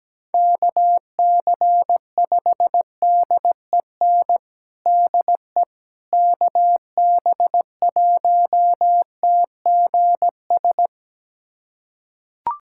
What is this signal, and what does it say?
Morse code, heard as a keyed tone, about 17 wpm, 700 Hz. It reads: KC5DEN DE KB1TGS
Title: USA Callsign DE Callsign - 6 Characters - 17wpm